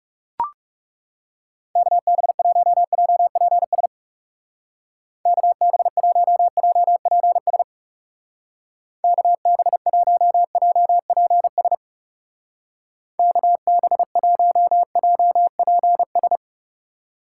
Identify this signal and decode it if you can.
Morse code, heard as a keyed tone, about 38 wpm, 700 Hz. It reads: KB1JPS KB1JPS KB1JPS KB1JPS